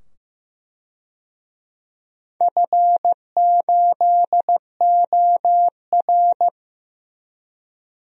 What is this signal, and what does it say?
Morse code, heard as a keyed tone, about 15 wpm, 700 Hz. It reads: F8OR
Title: DXCC-100 Call Signs - 4 Characters - 15wpm